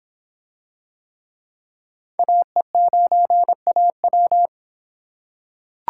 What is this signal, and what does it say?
Morse code, heard as a keyed tone, about 24 wpm, 700 Hz. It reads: AE9AW